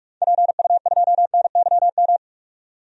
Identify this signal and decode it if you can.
Morse code, heard as a keyed tone, about 44 wpm, 700 Hz. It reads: PU2NYM
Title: DXCC-100 Call Signs - 6 Characters - 45wpm